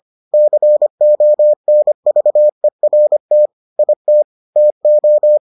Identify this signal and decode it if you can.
Morse code, heard as a keyed tone, about 25 wpm, 600 Hz. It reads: CONVERT IT TO